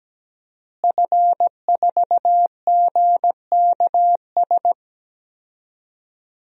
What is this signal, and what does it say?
Morse code, heard as a keyed tone, about 17 wpm, 700 Hz. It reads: F4GKS
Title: DXCC-100 Call Signs - 5 Characters - Repeated 2x - 17wpm